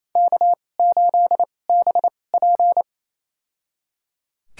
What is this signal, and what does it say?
Morse code, heard as a keyed tone, about 28 wpm, spaced as for 20 wpm, 700 Hz. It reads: K8BP
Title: USA Call Signs - 2x Character Spacing - 28wpm